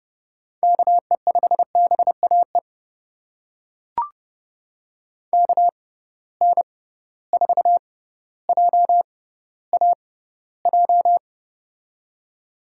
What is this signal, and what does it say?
Morse code, heard as a keyed tone, about 30 wpm, spaced as for 15 wpm, 700 Hz. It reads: KE5BAE KN4JAJ
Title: USA Call Signs - 6 Characters - 6x Character Spacing - 30wpm